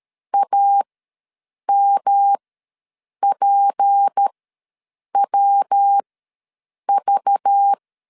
Morse code, heard as a keyed tone, about 13 wpm, spaced as for 7 wpm, 800 Hz. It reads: AMPWV
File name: LEZIONI / 7 GF9Z / 7 GF ws 12wpm 3140